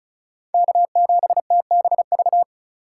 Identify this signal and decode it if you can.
Morse code, heard as a keyed tone, about 36 wpm, 700 Hz. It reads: K7TBV
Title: USA Call Signs - 5 Characters - 4x Character Spacing - 35wpm